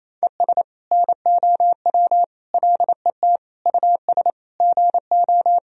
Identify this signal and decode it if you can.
Morse code, heard as a keyed tone, about 28 wpm, 700 Hz. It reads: ES NOW LET US GO
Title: Sentences from Top 300 Words and Abbreviations 28wpm